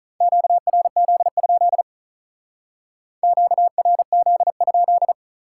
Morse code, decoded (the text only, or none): QRZ? QRZ?